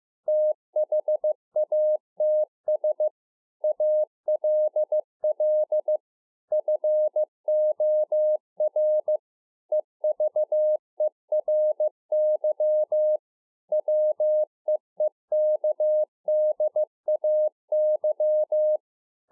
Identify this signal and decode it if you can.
Morse code, heard as a keyed tone, about 15 wpm, 610 Hz. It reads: THATS ALL FOR EVERY WEEKDAY
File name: 15WPM